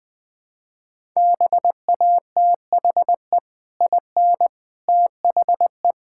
Morse code, heard as a keyed tone, about 20 wpm, 700 Hz. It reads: BATHE IN THE
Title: Easy Sentences with Hints - 20wpm